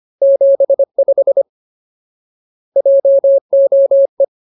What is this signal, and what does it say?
Morse code, heard as a keyed tone, about 25 wpm, 550 Hz. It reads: 75 JOE
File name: Ultimate Word List / 25WPM Ultimate